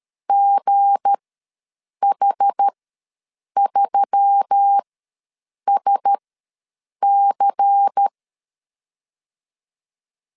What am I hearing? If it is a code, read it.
Morse code, heard as a keyed tone, about 13 wpm, spaced as for 7 wpm, 800 Hz. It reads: GH3SC
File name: LEZIONI / 9 286 ws 12wpm 3351